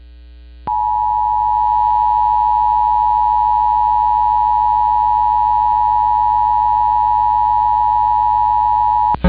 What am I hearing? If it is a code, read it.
Emergency Alert System (SAME) tones (the 853 and 960 Hz attention tone). It is heard from 0.7 to 8.6 s.